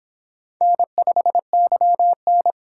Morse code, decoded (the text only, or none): N5YN